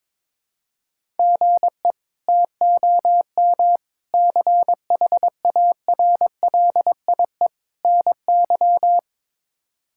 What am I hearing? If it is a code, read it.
Morse code, heard as a keyed tone, about 22 wpm, 700 Hz. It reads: GE TOM CHARLIE NY
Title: K1USN SST QSO Practice - 22wpm